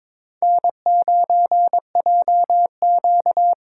Morse code, heard as a keyed tone, about 26 wpm, 700 Hz. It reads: N9JQ
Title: Speed Racing - USA 1x2 Call Signs 22wpm